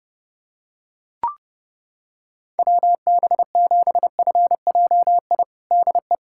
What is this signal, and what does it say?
Morse code, heard as a keyed tone, about 30 wpm, 700 Hz. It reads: WB7FJI DE
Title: USA Callsign DE Callsign - 6 Characters - 30wpm